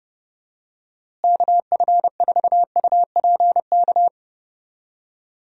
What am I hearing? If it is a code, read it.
Morse code, heard as a keyed tone, about 30 wpm, 700 Hz. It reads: KF4UPK
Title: USA Call Signs - 6 Characters - Repeated 3x - 30wpm